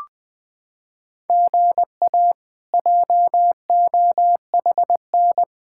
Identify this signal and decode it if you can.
Morse code, heard as a keyed tone, about 20 wpm, 700 Hz. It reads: GA JOHN
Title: K1USN SST QSO Practice - 20wpm